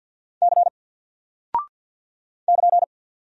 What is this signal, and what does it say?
Morse code, heard as a keyed tone, about 50 wpm, 700 Hz. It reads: C <KN>
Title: Single Character (Letter:Number:Punctuation:Prosign) 50wpm